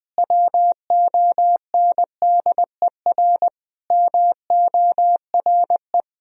Morse code, heard as a keyed tone, about 21 wpm, 700 Hz. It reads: WONDER MORE
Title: Sentences from Top 500 Words and Abbreviations 20wpm